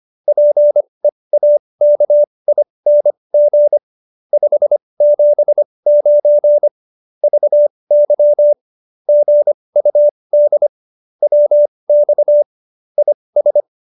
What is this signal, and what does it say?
Morse code, heard as a keyed tone, about 25 wpm, spaced as for 20 wpm, 590 Hz. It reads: PEAKING 579 VY GUD WX IS